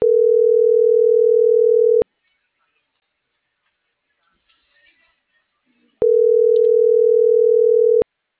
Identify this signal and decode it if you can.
Ringback tone (North America), 440 + 480 Hz.